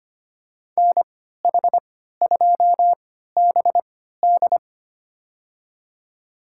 Morse code, heard as a keyed tone, about 25 wpm, spaced as for 14 wpm, 700 Hz. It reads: NH2BD
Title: USA Call Signs - 5 Characters - 3x Character Spacing - 25wpm